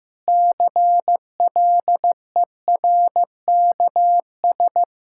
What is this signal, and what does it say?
Morse code, heard as a keyed tone, about 15 wpm, 700 Hz. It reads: CLERKS
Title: Pangram Challenge v2 - 15wpm